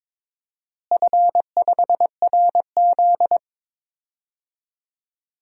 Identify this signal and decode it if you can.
Morse code, heard as a keyed tone, about 21 wpm, 700 Hz. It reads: F5RZ
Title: DXCC-100 Call Signs - 4 Characters - Repeated 3x - 22wpm